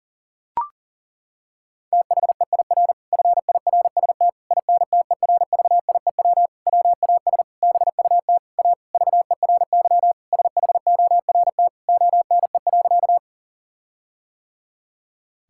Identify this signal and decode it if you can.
Morse code, heard as a keyed tone, about 40 wpm, 700 Hz. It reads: THEIR FIRST INTERVIEW WAS BUT A VERY SHORT ONE.
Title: Sentences from Top 1600 Words 40wpm